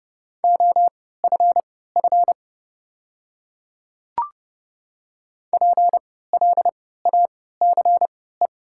Morse code, heard as a keyed tone, about 29 wpm, spaced as for 17 wpm, 700 Hz. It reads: OFF PLACE
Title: 3x Character Spacing - Top 200 Words - 30wpm